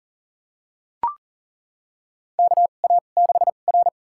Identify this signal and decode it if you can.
Morse code, heard as a keyed tone, about 40 wpm, spaced as for 29 wpm, 700 Hz. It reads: KA6R
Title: USA Call Signs - 2x Character Spacing - 40wpm